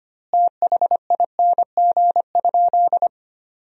Morse code, heard as a keyed tone, about 25 wpm, 700 Hz. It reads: THING?
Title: 2x Word-Spacing - Sentences from Top 200 Words - 25wpm